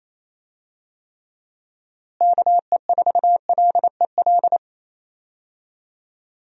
Morse code, heard as a keyed tone, about 28 wpm, 700 Hz. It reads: KE4LEL
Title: USA Call Signs - 6 Characters - Repeated 3x - 28wpm